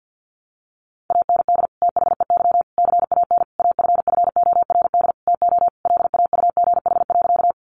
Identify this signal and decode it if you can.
Morse code, heard as a keyed tone, about 49 wpm, 700 Hz. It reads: AND THEY CAN AFFORD TO LAUGH.
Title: Sentences from Top 1100 Words 50wpm